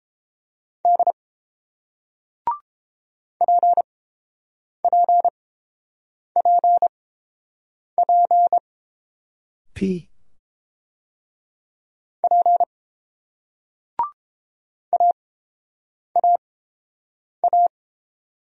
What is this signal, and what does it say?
Morse code, heard as a keyed tone, about 29 wpm, spaced as for 7 wpm, 700 Hz. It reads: D PPPP P AAA